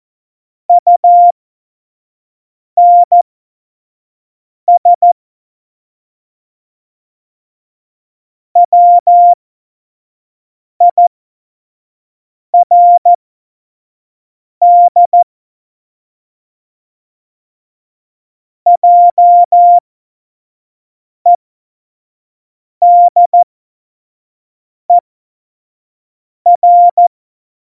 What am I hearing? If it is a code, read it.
Morse code, heard as a keyed tone, about 14 wpm, spaced as for 5 wpm, 700 Hz. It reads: UNS WIRD JEDER
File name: Morsekurs / Witze / Witz 12. November 2019, 25 F70